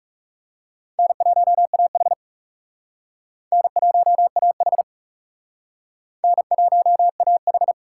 Morse code, heard as a keyed tone, about 38 wpm, 700 Hz. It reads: N1AH N1AH N1AH